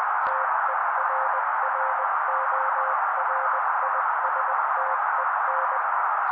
Morse code, heard as a keyed tone, about 20 wpm, 530 Hz. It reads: TERRORISTEN
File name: daily / 18wpm noise